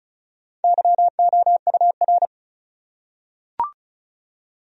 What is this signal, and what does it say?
Morse code, heard as a keyed tone, about 34 wpm, 700 Hz. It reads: YOUR
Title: Top 100 Words Repeated 35wpm